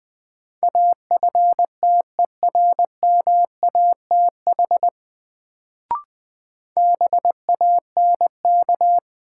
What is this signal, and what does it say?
Morse code, heard as a keyed tone, about 20 wpm, 700 Hz. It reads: AFTERMATH BANK
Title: Sets of 5 Words - New Word Drill - 20wpm